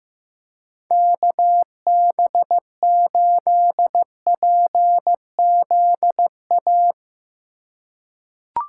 Morse code, heard as a keyed tone, about 15 wpm, 700 Hz. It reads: KB8PZA
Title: USA Call Signs - 6 Characters - 4x Character Spacing - 15wpm